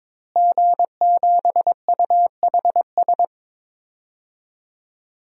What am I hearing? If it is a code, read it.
Morse code, heard as a keyed tone, about 22 wpm, 700 Hz. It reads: G7UHS
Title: DXCC-100 Call Signs - 5 Characters - Repeated 3x - 22wpm